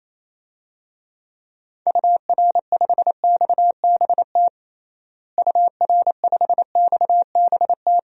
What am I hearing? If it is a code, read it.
Morse code, heard as a keyed tone, about 28 wpm, 700 Hz. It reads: UR5XBT UR5XBT
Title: DXCC-100 Call Signs - 6 Characters - Repeated 3x - 28wpm